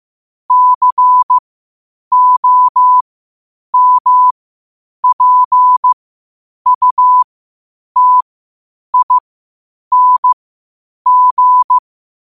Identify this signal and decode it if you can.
Morse code, heard as a keyed tone, about 15 wpm, spaced as for 8 wpm, 1000 Hz. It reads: COMPUTING
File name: qotdfiles / 8wpm